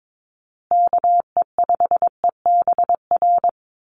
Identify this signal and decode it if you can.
Morse code, heard as a keyed tone, about 22 wpm, 700 Hz. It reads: KE5EBR
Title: USA Call Signs Repeated 2x 22wpm